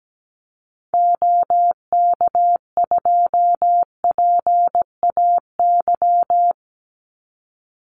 Morse code, heard as a keyed tone, about 17 wpm, 700 Hz. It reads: OK2PAY
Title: Call Signs Repeated 3x - 17wpm